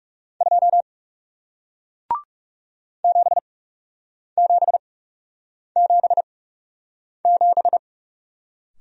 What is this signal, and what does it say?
Morse code, heard as a keyed tone, about 40 wpm, spaced as for 8 wpm, 700 Hz. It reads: 2 7777